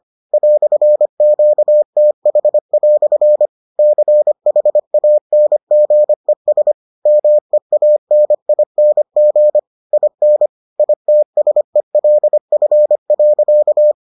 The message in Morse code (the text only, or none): "QTH" CHANGES MEANING IN ITSELF.